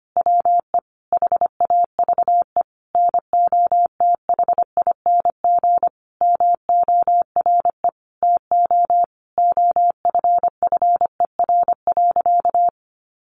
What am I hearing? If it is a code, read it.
Morse code, heard as a keyed tone, about 25 wpm, 700 Hz. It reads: WE HAVE NOTHING MORE TO OFFER.
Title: Sentences from Top 1100 Words 25wpm